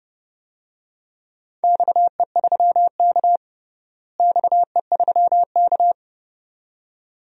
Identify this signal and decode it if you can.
Morse code, heard as a keyed tone, about 30 wpm, 700 Hz. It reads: XE3K XE3K
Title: DXCC-100 Call Signs - 4 Characters - Repeated 2x - 30wpm